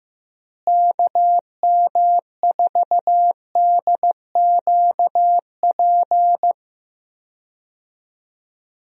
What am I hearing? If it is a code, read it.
Morse code, heard as a keyed tone, about 15 wpm, 700 Hz. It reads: KM4DQP
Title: USA Call Signs - 6 Characters - Repeated 3x - 15wpm